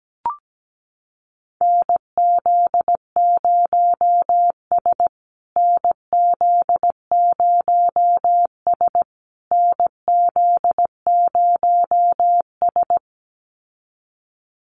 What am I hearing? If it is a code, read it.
Morse code, heard as a keyed tone, about 17 wpm, 700 Hz. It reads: NZ0S NZ0S NZ0S
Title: USA Call Signs Repeated 3x 17wpm